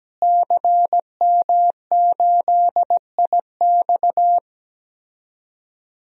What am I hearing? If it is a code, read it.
Morse code, heard as a keyed tone, about 17 wpm, 700 Hz. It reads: CM8IX
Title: DXCC-100 Call Signs - 5 Characters - Repeated 2x - 17wpm